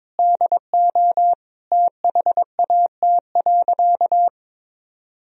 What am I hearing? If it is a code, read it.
Morse code, heard as a keyed tone, about 22 wpm, 700 Hz. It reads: DO THAT.
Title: Sentences from Top 1000 Words - Review Part 1 - 22wpm